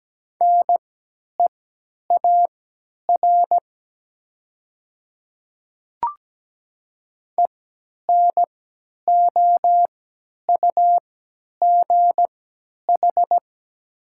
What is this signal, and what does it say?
Morse code, heard as a keyed tone, about 17 wpm, spaced as for 10 wpm, 700 Hz. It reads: NEAR ENOUGH
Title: 3x Character Spacing - Top 500 Words - 17wpm